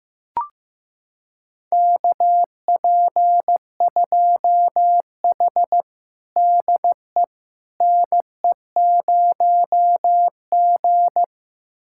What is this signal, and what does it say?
Morse code, heard as a keyed tone, about 16 wpm, 700 Hz. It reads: KP2H DE NE0G
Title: USA Callsign DE Callsign - 4 Characters - 15wpm